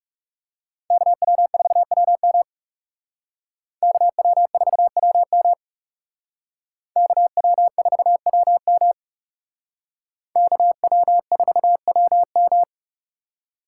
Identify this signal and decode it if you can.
Morse code, heard as a keyed tone, about 36 wpm, 700 Hz. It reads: KW4WM KW4WM KW4WM KW4WM
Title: Speed Racing - USA Call Signs 30wpm